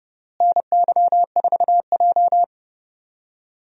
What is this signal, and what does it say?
Morse code, heard as a keyed tone, about 30 wpm, 700 Hz. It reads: NY4J
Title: USA Call Signs - 4 Characters - Repeated 3x - 30wpm